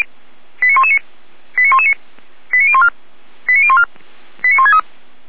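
Five-tone selective call (ZVEI), heard as quick runs of five tones, tones of about 70 ms. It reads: 89119 89119 89014 89014 89163
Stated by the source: ZVEI five-tone selective call received on SDR by fire brigade Feuerwehr Orlamünde.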